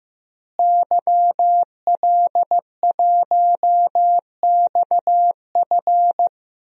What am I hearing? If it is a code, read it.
Morse code, heard as a keyed tone, about 15 wpm, 700 Hz. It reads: YL1XF
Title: DXCC-100 Call Signs - 5 Characters - Repeated 3x - 15wpm